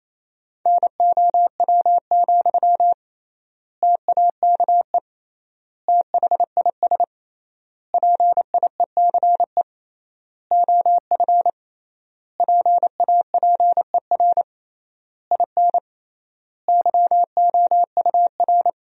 Morse code, heard as a keyed tone, about 28 wpm, 700 Hz. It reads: NOW, TAKE THIS PIECE OF PAPER IN YOUR
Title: 3x Word-Spacing - Sentences from Top 400 Words - 28wpm